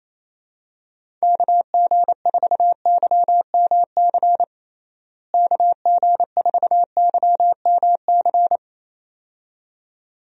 Morse code, heard as a keyed tone, about 28 wpm, 700 Hz. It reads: KG4YMC KG4YMC
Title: USA Call Signs - 6 Characters - Repeated 2x - 28wpm